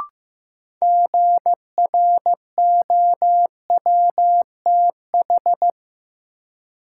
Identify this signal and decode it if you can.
Morse code, heard as a keyed tone, about 15 wpm, 700 Hz. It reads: GROWTH
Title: Sentences from Top 1000 Words - Review Part 5 - New Word Drill - 15wpm